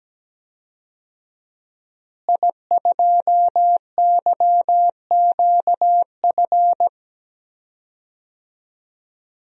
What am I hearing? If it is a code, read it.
Morse code, heard as a keyed tone, about 17 wpm, 700 Hz. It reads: I2YQF
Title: DXCC-100 Call Signs - 5 Characters - Repeated 3x - 17wpm